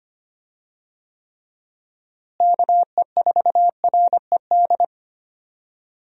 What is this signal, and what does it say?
Morse code, heard as a keyed tone, about 25 wpm, 700 Hz. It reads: KE4RED
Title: USA Call Signs - 6 Characters - Repeated 3x - 25wpm